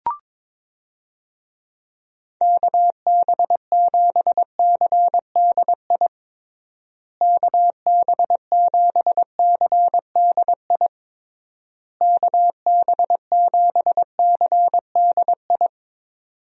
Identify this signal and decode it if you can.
Morse code, heard as a keyed tone, about 22 wpm, 700 Hz. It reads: KB7CDI KB7CDI KB7CDI
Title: USA Call Signs - 6 Characters - Repeated 3x - 22wpm